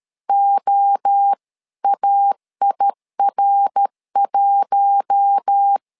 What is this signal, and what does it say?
Morse code, heard as a keyed tone, about 13 wpm, 800 Hz. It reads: OAIR1
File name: LEZIONI / 4 Q1S 12wpm 1802